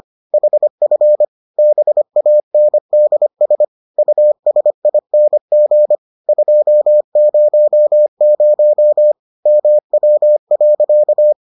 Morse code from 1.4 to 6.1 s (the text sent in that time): BANDS USING